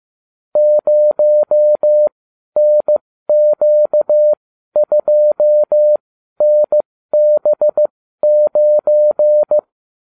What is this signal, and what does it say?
Morse code, heard as a keyed tone, about 15 wpm, spaced as for 12 wpm, 600 Hz. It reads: 0NQ2NB9